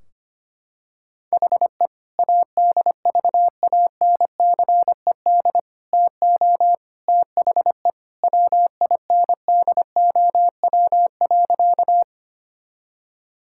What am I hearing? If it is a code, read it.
Morse code, heard as a keyed tone, about 25 wpm, 700 Hz. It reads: HE ADVANCED TO THE WINDOW.
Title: Sentences from Top 1600 Words 25wpm